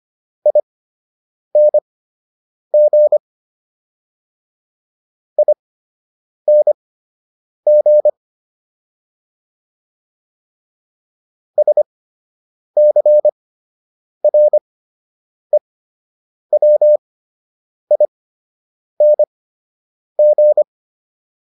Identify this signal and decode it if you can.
Morse code, heard as a keyed tone, about 25 wpm, spaced as for 8 wpm, 600 Hz. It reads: ING ING SCREWING